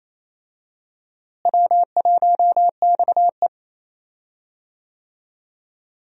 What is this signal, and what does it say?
Morse code, heard as a keyed tone, about 28 wpm, 700 Hz. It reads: W1XE
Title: USA Call Signs - 4 Characters - Repeated 3x - 28wpm